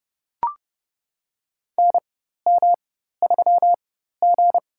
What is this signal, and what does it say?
Morse code, heard as a keyed tone, about 30 wpm, spaced as for 14 wpm, 700 Hz. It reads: NM3G
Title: USA Call Signs - 4 Characters - 4x Character Spacing - 30wpm